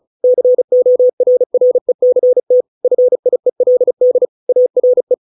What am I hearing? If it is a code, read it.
Morse code, heard as a keyed tone, about 34 wpm, 500 Hz. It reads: CORRECT FIELD ARE